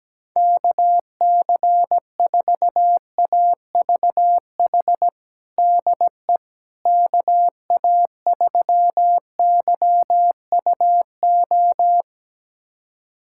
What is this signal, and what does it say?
Morse code, heard as a keyed tone, about 17 wpm, 700 Hz. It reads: KC4AVH DE KA3YUO
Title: USA Callsign DE Callsign - 6 Characters - 17wpm